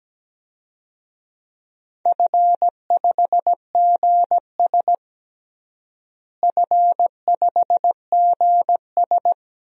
Morse code, heard as a keyed tone, about 17 wpm, 700 Hz. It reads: F5GS F5GS
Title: DXCC-100 Call Signs - 4 Characters - Repeated 2x - 17wpm